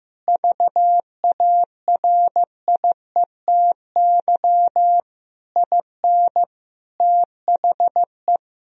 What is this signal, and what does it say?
Morse code, heard as a keyed tone, about 16 wpm, 700 Hz. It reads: VARIETY IN THE
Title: Sentences from Top 1000 Words - Review Part 4 - 15wpm